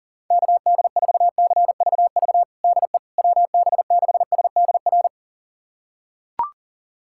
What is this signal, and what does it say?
Morse code, heard as a keyed tone, about 37 wpm, 700 Hz. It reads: KD4CVV DE WB6SDR
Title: USA Callsign DE Callsign - 6 Characters - 40wpm